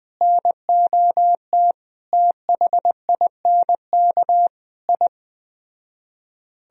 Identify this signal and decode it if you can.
Morse code, heard as a keyed tone, about 20 wpm, 700 Hz. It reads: NOT THINK I